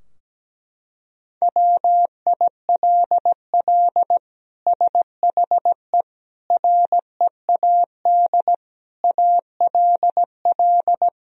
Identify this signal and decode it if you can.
Morse code, heard as a keyed tone, about 17 wpm, 700 Hz. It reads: WILL SHE READ ALL
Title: Sentences from Top 1000 Words - Review Part 4 - 17wpm